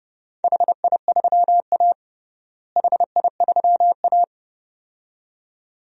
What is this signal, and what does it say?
Morse code, heard as a keyed tone, about 30 wpm, 700 Hz. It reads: HI3A HI3A
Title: DXCC-100 Call Signs - 4 Characters - Repeated 2x - 30wpm